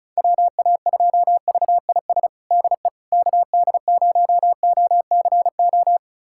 Morse code, heard as a keyed tone, about 36 wpm, 700 Hz. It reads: WA2VIS DE KD0OCO
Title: USA Callsign DE Callsign - 6 Characters - 35wpm